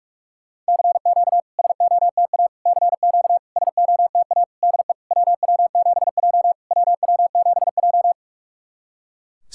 Morse code, heard as a keyed tone, about 46 wpm, 690 Hz. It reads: CQ SOTA CQ SOTA DE WW7J WW7J
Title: SOTA QSO Practice 45wpm